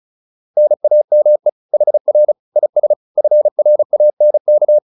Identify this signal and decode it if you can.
Morse code, heard as a keyed tone, about 34 wpm, 600 Hz. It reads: NAME HR IS FRANK